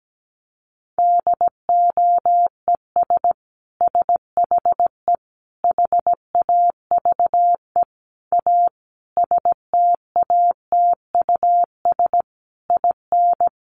Morse code, heard as a keyed tone, about 17 wpm, 700 Hz. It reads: DOES SHE HAVE A STATUS IN
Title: Sentences from Top 1400 Words 17wpm